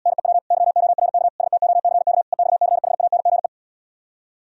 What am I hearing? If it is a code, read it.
Morse code, heard as a keyed tone, about 77 wpm, 700 Hz. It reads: NEW YORK STOCK EXCHANGE